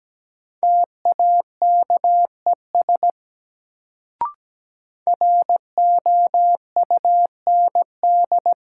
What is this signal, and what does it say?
Morse code, heard as a keyed tone, about 17 wpm, 700 Hz. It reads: TAKES ROUND
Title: Sets of 5 Words Encore - New Word Drill - 17wpm